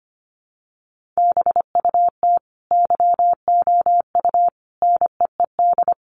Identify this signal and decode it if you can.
Morse code, heard as a keyed tone, about 25 wpm, 700 Hz. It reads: BUT YOU NEED
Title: Sentences from Top 1300 Words 25wpm